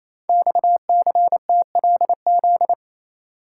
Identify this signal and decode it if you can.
Morse code, heard as a keyed tone, about 28 wpm, 700 Hz. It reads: XCTLZ
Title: Code Groups - 5 Letters - 28wpm